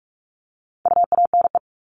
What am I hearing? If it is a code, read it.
Morse code, heard as a keyed tone, about 44 wpm, 700 Hz. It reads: UANE
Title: All Cut Numbers - 4-Digit 45wpm